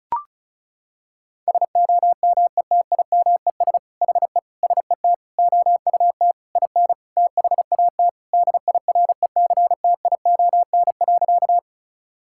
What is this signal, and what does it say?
Morse code, heard as a keyed tone, about 34 wpm, 700 Hz. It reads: SOMETIMES HE SET OUT IN THAT DIRECTION.
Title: Sentences from Top 1000 Words - Review Part 5 - 35wpm